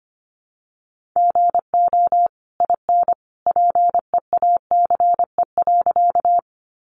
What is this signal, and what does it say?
Morse code, heard as a keyed tone, about 25 wpm, 700 Hz. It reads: GO IN PEACE.
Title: Sentences from Top 1500 Words 25wpm